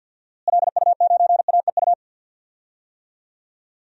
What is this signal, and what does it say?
Morse code, heard as a keyed tone, about 52 wpm, 700 Hz. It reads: LU9AEU
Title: DXCC-100 Call Signs - 6 Characters - Repeated 2x - 50wpm